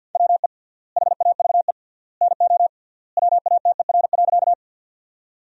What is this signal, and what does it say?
Morse code, heard as a keyed tone, about 50 wpm, 700 Hz. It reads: WE HAVE NO WATER.
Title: 3x Word-Spacing - Sentences from Top 100 Words - 50wpm